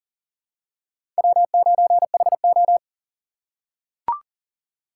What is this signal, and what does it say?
Morse code, heard as a keyed tone, about 40 wpm, 700 Hz. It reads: W9HO
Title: USA Call Signs - 4 Characters - 4x Character Spacing - 40wpm